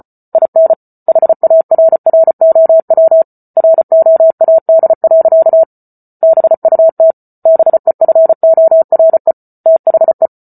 Morse code, heard as a keyed tone, about 34 wpm, 650 Hz. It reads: IN HARROW ROAD. BUT BEFORE THE